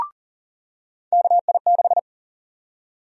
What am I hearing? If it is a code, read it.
Morse code, heard as a keyed tone, about 40 wpm, 700 Hz. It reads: KI6
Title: USA Call Sign Prefixes - 40wpm